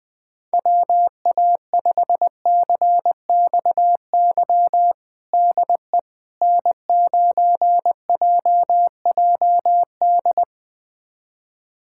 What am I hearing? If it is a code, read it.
Morse code, heard as a keyed tone, about 20 wpm, 700 Hz. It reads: WA5CXY DE N9JJD
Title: USA Callsign DE Callsign 20wpm